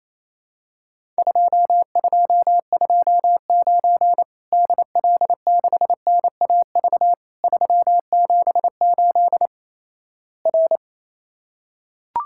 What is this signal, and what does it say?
Morse code, heard as a keyed tone, about 28 wpm, 700 Hz. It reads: 2229 DL6NAV 378 R
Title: Worked All Europe DX Contest - QTC Exchange - 28wpm